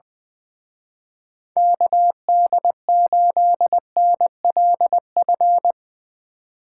Morse code, heard as a keyed tone, about 20 wpm, 700 Hz. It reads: KD8NLF
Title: Speed Racing - USA 2x3 Call Signs 20wpm